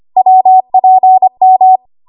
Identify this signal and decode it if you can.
Morse code, heard as a keyed tone, about 25 wpm, 750 Hz. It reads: WPM
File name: Archive / 25 WPM / 25WPM